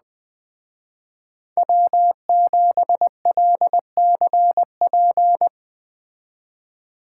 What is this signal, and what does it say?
Morse code, heard as a keyed tone, about 20 wpm, 700 Hz. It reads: W7LCP
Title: Speed Racing - USA 1x3 Call Signs 17wpm